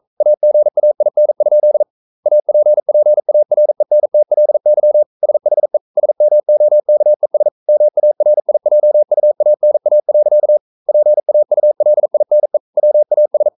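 Morse code, heard as a keyed tone, about 42 wpm, 600 Hz. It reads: AGAIN? APPARENTLY SHE SMOKES MARIJUANA. PAULINE WAS
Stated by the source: Morse code sending of an English novel at 42 wpm.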